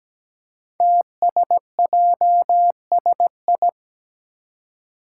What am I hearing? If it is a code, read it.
Morse code, heard as a keyed tone, about 17 wpm, 700 Hz. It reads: TSJSI
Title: Five Challenging Letter-Number Combinations 17wpm